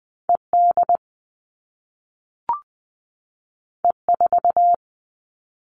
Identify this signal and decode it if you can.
Morse code, heard as a keyed tone, about 20 wpm, 700 Hz. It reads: ED E4
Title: All Cut Numbers - 2-Digit 20wpm